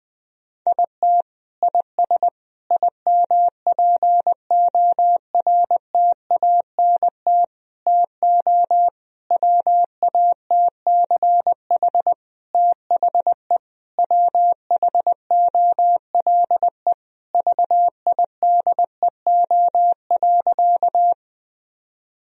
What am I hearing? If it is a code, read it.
Morse code, heard as a keyed tone, about 20 wpm, 700 Hz. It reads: IT IS IMPORTANT TO WATCH THE WHOLE VIDEO.
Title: Sentences from Top 1000 Words - Review Part 4 - 20wpm